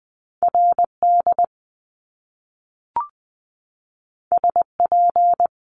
Morse code, heard as a keyed tone, about 20 wpm, 700 Hz. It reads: RD SP
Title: Top Two Letter Combinations in Words 20wpm